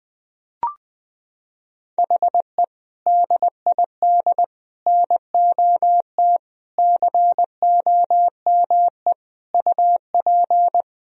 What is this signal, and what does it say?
Morse code, heard as a keyed tone, about 20 wpm, 700 Hz. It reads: HE DID NOT COME UP
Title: Sentences from Top 1000 Words - Review Part 3 - 20wpm